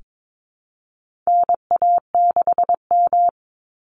Morse code, heard as a keyed tone, about 22 wpm, 700 Hz. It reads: NA6M
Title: Call Signs Repeated 3x - 22wpm